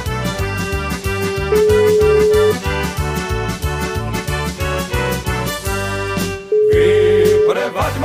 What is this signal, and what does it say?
Ringback tone (Europe, 425 Hz), with noise.